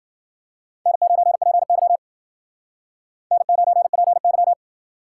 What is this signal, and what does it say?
Morse code, heard as a keyed tone, about 53 wpm, 700 Hz. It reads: N9PY N9PY